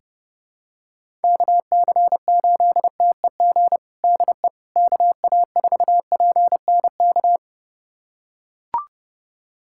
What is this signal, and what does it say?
Morse code, heard as a keyed tone, about 30 wpm, 700 Hz. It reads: KC8TEG DE KA4PNK
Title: USA Callsign DE Callsign 30wpm